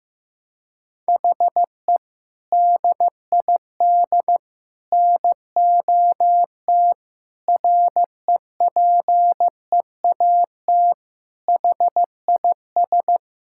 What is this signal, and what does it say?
Morse code, heard as a keyed tone, about 15 wpm, 700 Hz. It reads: HE DID NOT REPEAT HIS
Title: Sentences from Top 1600 Words 15wpm